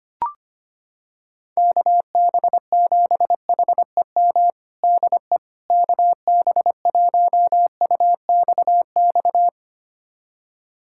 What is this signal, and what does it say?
Morse code, heard as a keyed tone, about 25 wpm, 700 Hz. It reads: KB7HEM DE KB1UXX
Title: USA Callsign DE Callsign - 6 Characters - 25wpm